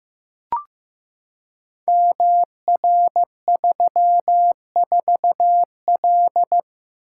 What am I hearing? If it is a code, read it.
Morse code, heard as a keyed tone, about 15 wpm, 700 Hz. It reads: MR34L
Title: Code Groups - 5 Letter:Numbers - 15wpm